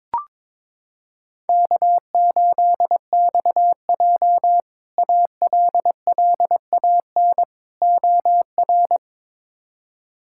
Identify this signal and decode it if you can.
Morse code, heard as a keyed tone, about 22 wpm, 700 Hz. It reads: K8XJ ALLAN OR
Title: K1USN SST QSO Practice - 22wpm